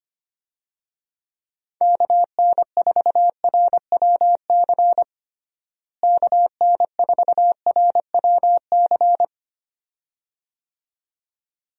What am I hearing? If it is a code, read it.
Morse code, heard as a keyed tone, about 25 wpm, 700 Hz. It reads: KN4RWC KN4RWC
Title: USA Call Signs - 6 Characters - Repeated 2x - 25wpm